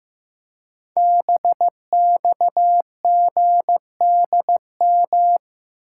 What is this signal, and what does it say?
Morse code, heard as a keyed tone, about 15 wpm, 700 Hz. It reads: BXGDM
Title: Code Groups - 5 Letters - 15wpm